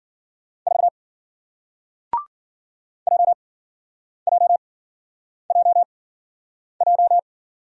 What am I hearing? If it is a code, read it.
Morse code, heard as a keyed tone, about 53 wpm, spaced as for 9 wpm, 700 Hz. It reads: 4 JJJJ